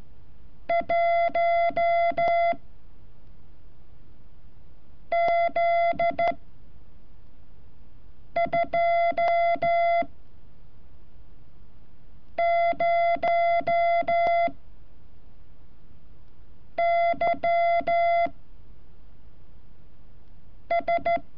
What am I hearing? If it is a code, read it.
Morse code, heard as a keyed tone, about 14 wpm, 680 Hz. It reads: 1 Z 2 0 Y S
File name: curso / ejercicio 31